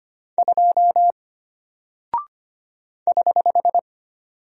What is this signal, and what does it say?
Morse code, heard as a keyed tone, about 25 wpm, 700 Hz. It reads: 2 <HH>
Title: Single Character (Letter:Number:Punctuation:Prosign) Alt 25wpm